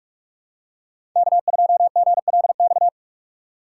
Morse code, heard as a keyed tone, about 48 wpm, 700 Hz. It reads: K2GLX
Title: USA Call Signs - 5 Characters - 3x Character Spacing - 45wpm